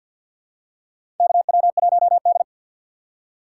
Morse code, heard as a keyed tone, about 47 wpm, 700 Hz. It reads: KW1D
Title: USA Call Signs - 4 Characters - 6x Character Spacing - 50wpm